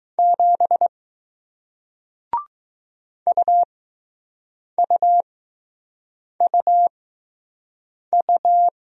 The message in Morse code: 7 UUUU

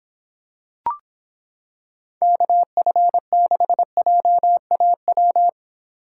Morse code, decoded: KF6JAW

